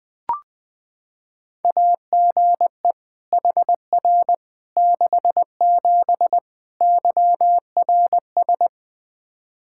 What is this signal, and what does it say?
Morse code, heard as a keyed tone, about 21 wpm, 700 Hz. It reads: AGE HR 67 YRS
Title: Practice QSO - Exchange 3 - More Info - 20wpm